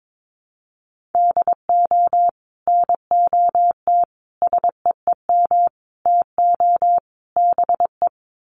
Morse code, heard as a keyed tone, about 22 wpm, 700 Hz. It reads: DO NOT SEEM TO BE
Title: Sets of 5 Words Encore2 22wpm